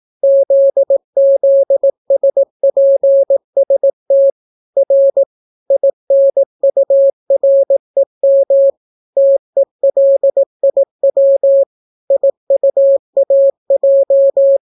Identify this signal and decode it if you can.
Morse code, heard as a keyed tone, about 18 wpm, 550 Hz. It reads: ZZSPST R INUREM TELIW IUAJ